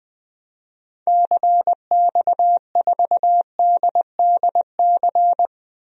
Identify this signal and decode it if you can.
Morse code, heard as a keyed tone, about 21 wpm, 700 Hz. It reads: CX4DDC
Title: DXCC-100 Call Signs - 6 Characters - Repeated 3x - 20wpm